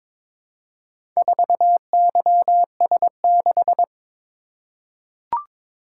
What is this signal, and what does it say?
Morse code, heard as a keyed tone, about 23 wpm, 700 Hz. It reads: 4YS6
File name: practice / 22wpm / Four Letter-Number Combinations - 22wpm